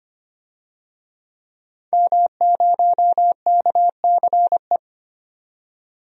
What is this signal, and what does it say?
Morse code, heard as a keyed tone, about 25 wpm, 700 Hz. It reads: M0KCE
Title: DXCC-100 Call Signs - 5 Characters - 25wpm